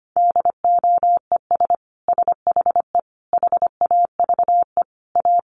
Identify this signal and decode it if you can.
Morse code, heard as a keyed tone, about 25 wpm, 700 Hz. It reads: DOES SHE HAVE A
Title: Sentences from Top 1500 Words 25wpm